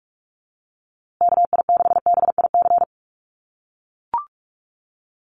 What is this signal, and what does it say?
Morse code, heard as a keyed tone, about 44 wpm, 700 Hz. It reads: KI6BIC
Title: USA Call Signs Repeated 3x 45wpm